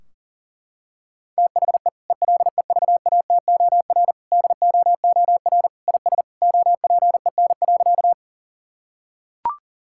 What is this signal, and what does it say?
Morse code, heard as a keyed tone, about 40 wpm, 700 Hz. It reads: THE ELEVATOR DOOR IS OPEN.
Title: Sentences from Top 1600 Words 40wpm